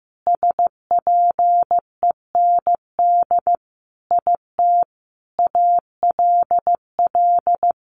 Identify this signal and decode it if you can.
Morse code, heard as a keyed tone, about 15 wpm, 700 Hz. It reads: SPEND IT ALL
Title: Sentences from Top 800 Words 15wpm